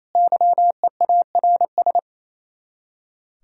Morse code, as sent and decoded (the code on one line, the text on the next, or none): -.-- . .- .-. ...
YEARS